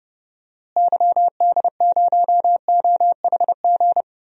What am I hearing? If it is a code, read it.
Morse code, heard as a keyed tone, about 30 wpm, 700 Hz. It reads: YD0OHG
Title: DXCC-100 Call Signs - 6 Characters - Repeated 2x - 30wpm